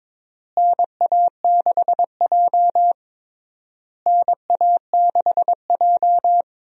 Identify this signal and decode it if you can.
Morse code, heard as a keyed tone, about 22 wpm, 700 Hz. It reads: NA6J NA6J
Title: USA Call Signs - 4 Characters - Repeated 3x - 22wpm